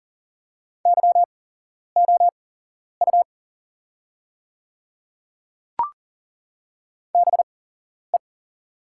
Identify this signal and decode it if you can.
Morse code, heard as a keyed tone, about 40 wpm, spaced as for 11 wpm, 690 Hz. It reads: YOU BE
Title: 8x Character Spacing - Top 100 Words - 40wpm